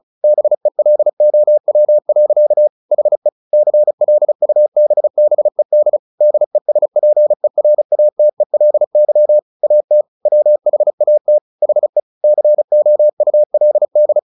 Morse code, heard as a keyed tone, about 35 wpm, 600 Hz. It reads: BELOW. HE CLUBBED DESPERATELY AT WHAT HE COULD
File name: the old man and the sea 2 35wpm